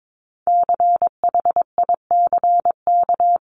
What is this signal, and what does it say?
Morse code, heard as a keyed tone, about 22 wpm, 700 Hz. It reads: CHICK
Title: CWT Exchange 22wpm